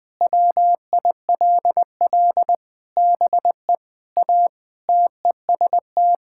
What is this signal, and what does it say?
Morse code, heard as a keyed tone, about 20 wpm, 700 Hz. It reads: WILL BE A TEST
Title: Sentences with 3-Word Phrases - 20wpm